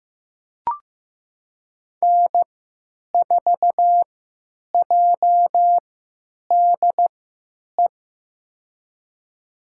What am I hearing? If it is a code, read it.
Morse code, heard as a keyed tone, about 15 wpm, spaced as for 9 wpm, 700 Hz. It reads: N4JDE
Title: USA Call Signs - 5 Characters - 3x Character Spacing - 15wpm